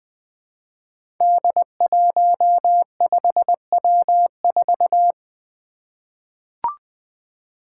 Morse code, heard as a keyed tone, about 20 wpm, 700 Hz. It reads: D15W4
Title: Five Challenging Letter-Number Combinations 20wpm